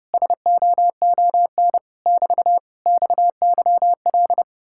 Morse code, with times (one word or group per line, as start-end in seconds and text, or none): 0.1-1.8 SOON
2.1-2.6 <BT>
2.9-4.4 XYL